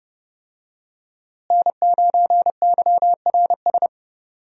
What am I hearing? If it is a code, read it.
Morse code, heard as a keyed tone, about 30 wpm, 700 Hz. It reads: N9YRS
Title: USA Call Signs - 5 Characters - Repeated 3x - 30wpm